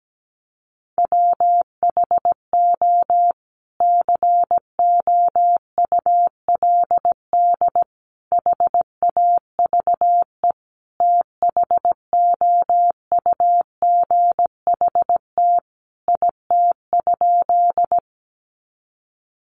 Morse code, heard as a keyed tone, about 17 wpm, 700 Hz. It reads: WHO COULD HAVE THOUGHT IT?
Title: Sentences from Top 300 Words 17wpm